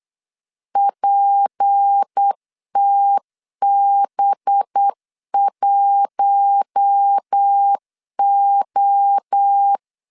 Morse code, heard as a keyed tone, about 8 wpm, 800 Hz. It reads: PTB1O
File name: LEZIONI / 6 PKX0 / 6 PK 8wpm 3003